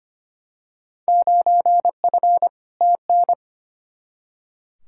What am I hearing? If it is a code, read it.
Morse code, heard as a keyed tone, about 25 wpm, 700 Hz. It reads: 9F TN
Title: Field Day Exchanges Encore - 25wpm